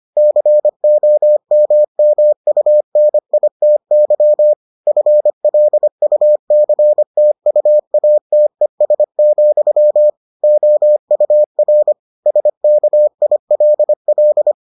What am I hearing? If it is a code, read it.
Morse code, heard as a keyed tone, about 25 wpm, 600 Hz. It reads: COMMUNITY FLUCTUATES, OUR SKILL